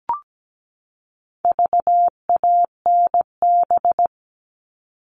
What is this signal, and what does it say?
Morse code, heard as a keyed tone, about 17 wpm, 700 Hz. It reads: VANB
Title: All Cut Numbers - 4-Digit 17wpm